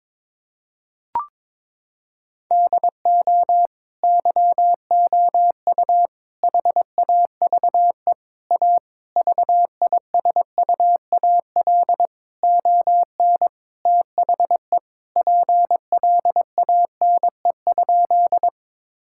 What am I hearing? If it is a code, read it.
Morse code, heard as a keyed tone, about 22 wpm, 700 Hz. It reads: DO YOU HAVE A VISUAL ON THE PLANE?
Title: Sentences from Top 1000 Words - Review Part 6 - 22wpm